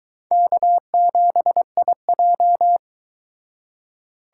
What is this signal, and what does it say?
Morse code, heard as a keyed tone, about 22 wpm, 700 Hz. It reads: K7IJ